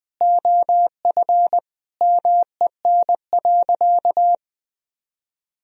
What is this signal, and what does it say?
Morse code, heard as a keyed tone, about 20 wpm, 700 Hz. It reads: OF MEN.
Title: Sentences from Top 1000 Words - Review Part 1 - 20wpm